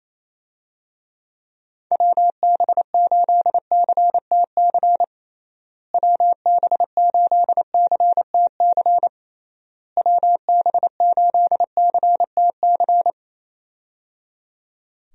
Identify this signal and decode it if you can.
Morse code, heard as a keyed tone, about 28 wpm, 700 Hz. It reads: WB8CTC WB8CTC WB8CTC
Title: USA Call Signs - 6 Characters - Repeated 3x - 28wpm